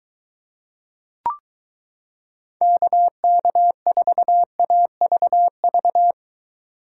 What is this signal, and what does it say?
Morse code, heard as a keyed tone, about 23 wpm, 700 Hz. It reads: KK4AVV